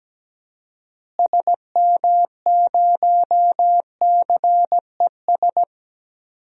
Morse code, heard as a keyed tone, about 17 wpm, 700 Hz. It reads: SM0CES
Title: DXCC-100 Call Signs - 6 Characters - Repeated 2x - 17wpm